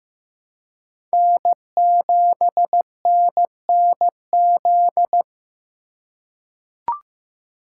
Morse code, heard as a keyed tone, about 15 wpm, 700 Hz. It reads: N7NNZ
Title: USA Call Signs - 3x Character Spacing - 15wpm